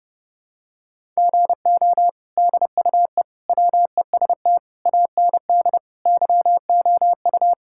Morse code, heard as a keyed tone, about 32 wpm, 700 Hz. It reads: GO DUE WEST AND YOU
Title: Sentences from Top 1000 Words - Review Part 6 - 30wpm